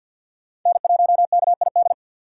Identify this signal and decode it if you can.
Morse code, heard as a keyed tone, about 53 wpm, 700 Hz. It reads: N1KID
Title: USA Call Signs - 5 Characters - 4x Character Spacing - 50wpm